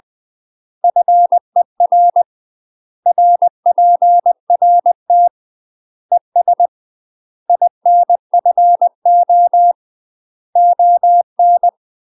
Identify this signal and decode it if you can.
Morse code, heard as a keyed tone, about 20 wpm, 700 Hz. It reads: FER RPRT ES INFO ON
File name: UK MASTER QSOs TRAINING / 20WPM / UK MASTER QSOs 0065